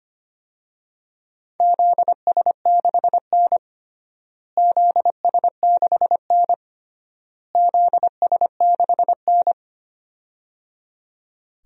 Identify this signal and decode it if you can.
Morse code, heard as a keyed tone, about 25 wpm, 700 Hz. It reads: ZS6N ZS6N ZS6N
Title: DXCC-100 Call Signs - 4 Characters - Repeated 3x - 25wpm